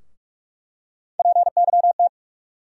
Morse code, heard as a keyed tone, about 47 wpm, 700 Hz. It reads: PYT
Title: USA Call Sign Postfixes - 3 Letters 45wpm